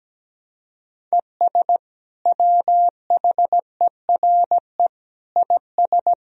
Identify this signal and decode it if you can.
Morse code, heard as a keyed tone, about 17 wpm, 700 Hz. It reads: ES WHERE IS
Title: Sentences from Top 200 Words and Abbreviations 17wpm